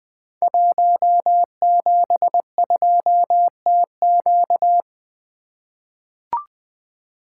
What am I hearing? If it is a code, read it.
Morse code, heard as a keyed tone, about 20 wpm, 700 Hz. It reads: 172TQ
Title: Code Groups - 5 Letter:Numbers - 20wpm